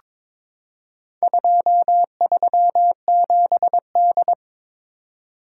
Morse code, heard as a keyed tone, about 22 wpm, 700 Hz. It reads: 237D
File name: practice / 22wpm / Four Letter-Number Combinations - 22wpm